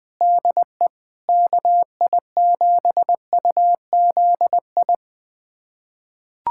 Morse code, heard as a keyed tone, about 21 wpm, 700 Hz. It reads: DE KI7UZI
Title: USA Callsign DE Callsign - 6 Characters - 20wpm